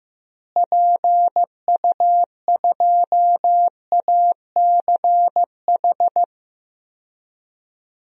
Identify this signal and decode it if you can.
Morse code, heard as a keyed tone, about 16 wpm, 700 Hz. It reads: PU2ACH
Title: DXCC-100 Call Signs - 6 Characters - Repeated 3x - 15wpm